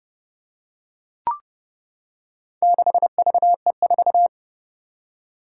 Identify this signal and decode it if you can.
Morse code, heard as a keyed tone, about 30 wpm, 700 Hz. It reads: 6VE4
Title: Four Challenging Letter-Number Combinations 30wpm